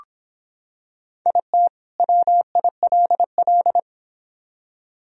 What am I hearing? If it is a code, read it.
Morse code, heard as a keyed tone, about 26 wpm, 700 Hz. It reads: IT WILL